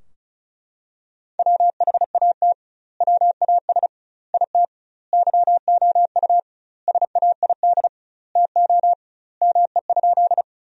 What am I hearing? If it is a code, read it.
Morse code, heard as a keyed tone, about 34 wpm, 700 Hz. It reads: WHAT WAS IT YOU SAID TO ME?
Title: 2x Word-Spacing - Sentences from Top 200 Words - 35wpm